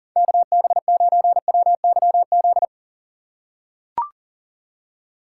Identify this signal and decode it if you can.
Morse code, heard as a keyed tone, about 39 wpm, 700 Hz. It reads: KB9WYZ
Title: USA Call Signs - 4x Character Spacing - 40wpm